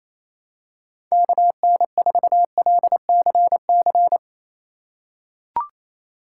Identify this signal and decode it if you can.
Morse code, heard as a keyed tone, about 28 wpm, 700 Hz. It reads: KN4LCC
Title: USA Call Signs - 6 Characters - 6x Character Spacing - 28wpm